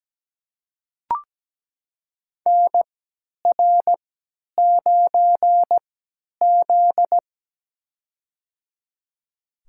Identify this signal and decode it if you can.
Morse code, heard as a keyed tone, about 17 wpm, spaced as for 10 wpm, 700 Hz. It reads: NR9Z
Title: USA Call Signs - 4 Characters - 3x Character Spacing - 17wpm